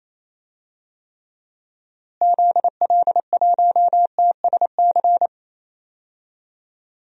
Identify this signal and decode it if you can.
Morse code, heard as a keyed tone, about 27 wpm, 700 Hz. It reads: ZL1TSC
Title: DXCC-100 Call Signs - 6 Characters - Repeated 3x - 28wpm